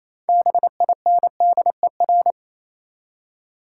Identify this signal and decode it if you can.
Morse code, heard as a keyed tone, about 28 wpm, 700 Hz. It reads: BINDER
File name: Easy Words - Speed-Racing with Hints - 22wpm